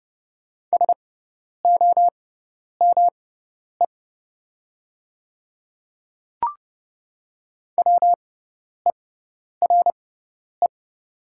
Morse code, heard as a keyed tone, about 30 wpm, spaced as for 10 wpm, 700 Hz. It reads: SOME WERE
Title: 6x Character Spacing - Top 200 Words - 30wpm